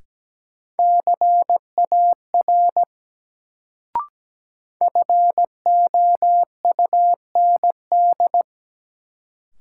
Morse code, heard as a keyed tone, about 17 wpm, 700 Hz. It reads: CAR FOUND
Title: Sets of 5 Words Encore - New Word Drill - 17wpm